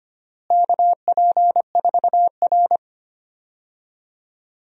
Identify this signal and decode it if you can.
Morse code, heard as a keyed tone, about 25 wpm, 700 Hz. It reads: KP4R
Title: DXCC-100 Call Signs - 4 Characters - Repeated 3x - 25wpm